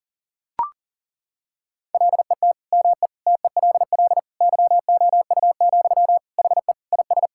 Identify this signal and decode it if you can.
Morse code, heard as a keyed tone, about 37 wpm, 700 Hz. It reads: LET ME TELL YOU, HE IS
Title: Sentences from Top 1600 Words 40wpm